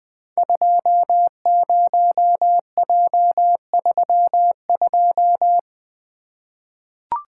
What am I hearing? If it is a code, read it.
Morse code, heard as a keyed tone, about 20 wpm, 700 Hz. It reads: 20J32
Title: Code Groups - 5 Letter:Numbers - 20wpm